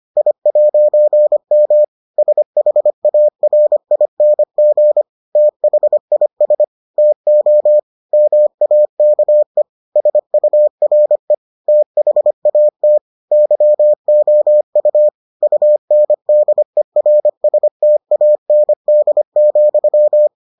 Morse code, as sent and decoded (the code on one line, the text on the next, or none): .. .----. -- / ... .... .- .-. .. -. --. / - .... .. ... / - --- / -- .- -.- . / ... ..- .-. / - .... .- - / -.-- --- ..- / ..- -. -.. .-. ... - .- -. -.. --..--
I'M SHARING THIS TO MAKE SURE THAT YOU UNDERSTAND,